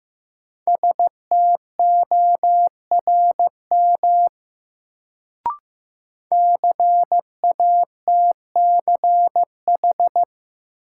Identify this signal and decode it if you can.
Morse code, heard as a keyed tone, about 15 wpm, 700 Hz. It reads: STORM CATCH
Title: Sentences from Top 1500 Words - New Word Drill - 15wpm